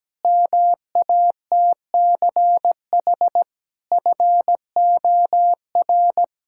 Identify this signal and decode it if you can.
Morse code, heard as a keyed tone, about 17 wpm, 700 Hz. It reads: MATCH FOR
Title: Sentences from Top 1600 Words 17wpm